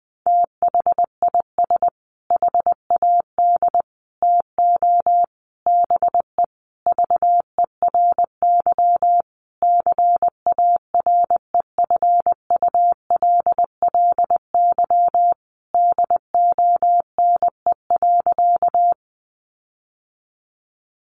Morse code, heard as a keyed tone, about 20 wpm, 700 Hz. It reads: THIS HAD TO BE VERY CAREFULLY DONE.
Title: Sentences from Top 1300 Words 20wpm